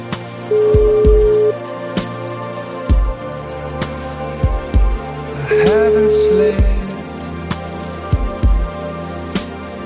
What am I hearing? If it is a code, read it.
Ringback tone (Europe, 425 Hz), with noise.